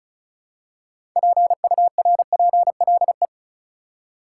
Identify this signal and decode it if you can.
Morse code, heard as a keyed tone, about 36 wpm, 700 Hz. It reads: PURPLE
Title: Six Letter Words - 35wpm